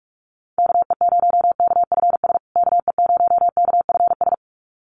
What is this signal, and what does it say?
Morse code, heard as a keyed tone, about 44 wpm, 700 Hz. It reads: KE0KFS KE0KFS
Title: USA Call Signs Repeated 2x 45wpm